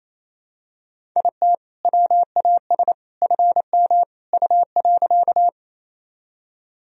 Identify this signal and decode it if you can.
Morse code, heard as a keyed tone, about 28 wpm, 700 Hz. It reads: IT WAS FM U.
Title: Sentences from Top 100 Words and Abbreviations 28wpm